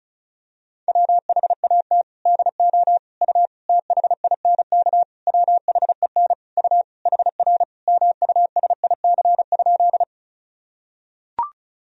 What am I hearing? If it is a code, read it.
Morse code, heard as a keyed tone, about 34 wpm, 700 Hz. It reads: WHAT DO U THINK WHEN U HR MUSIC?
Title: Sentences from Top 300 Words and Abbreviations 35wpm